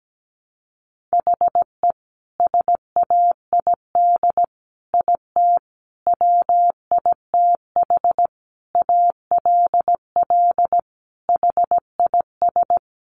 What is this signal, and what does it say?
Morse code, heard as a keyed tone, about 17 wpm, 700 Hz. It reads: HE SAID IT WITH ALL HIS
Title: Sentences from Top 1200 Words 17wpm